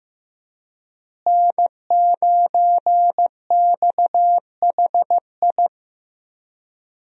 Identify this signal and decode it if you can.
Morse code, heard as a keyed tone, about 15 wpm, 700 Hz. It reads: N9XHI
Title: USA Call Signs - 5 Characters - Repeated 3x - 15wpm